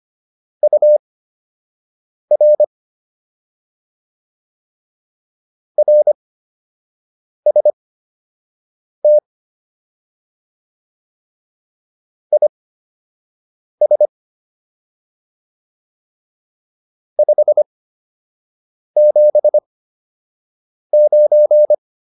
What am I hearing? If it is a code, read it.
Morse code, heard as a keyed tone, about 25 wpm, spaced as for 6 wpm, 600 Hz. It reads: UR RST IS 579